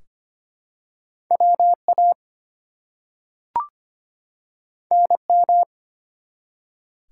Morse code, heard as a keyed tone, about 25 wpm, 700 Hz. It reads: WA NM
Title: US States and Canadian Provinces 25wpm